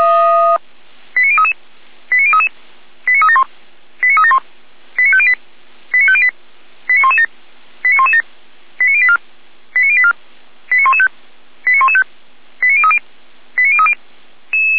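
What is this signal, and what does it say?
Five-tone selective call (ZVEI), heard as quick runs of five tones, tones of about 80 ms. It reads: 89033 89033 89371 89371 89508 89508 89117 89117 89084 89084 89115 89115 89030 89030
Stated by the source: ZVEI five-tone selective call received on an SDR by feuerwehr-orlamuende fire brigade.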